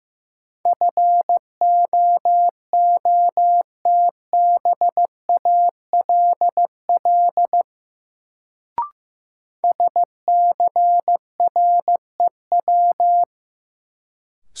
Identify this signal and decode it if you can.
Morse code, heard as a keyed tone, about 15 wpm, 700 Hz. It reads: FOOTBALL SCREW
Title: Sentences from Top 1600 Words - New Word Drill - 15wpm